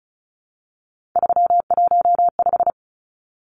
Morse code, heard as a keyed tone, about 34 wpm, 700 Hz. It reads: 315
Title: Numbers - 3-Digit 35wpm